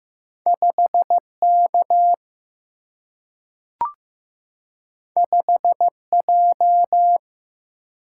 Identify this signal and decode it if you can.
Morse code, heard as a keyed tone, about 15 wpm, 700 Hz. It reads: 5 K 5 J
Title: Single-Letter Single-Number 15wpm